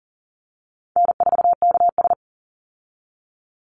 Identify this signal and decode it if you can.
Morse code, heard as a keyed tone, about 40 wpm, 700 Hz. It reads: N4KS
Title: Easy Call Signs Repeated 3x - 40wpm